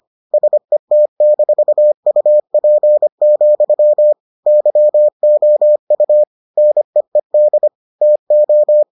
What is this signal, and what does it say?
Morse code, heard as a keyed tone, about 25 wpm, 600 Hz. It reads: SET-UP, YOU NEED TO